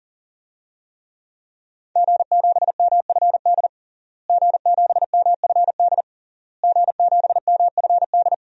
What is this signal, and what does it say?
Morse code, heard as a keyed tone, about 40 wpm, 700 Hz. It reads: G7MFD G7MFD G7MFD
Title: DXCC-100 Call Signs - 5 Characters - Repeated 3x - 40wpm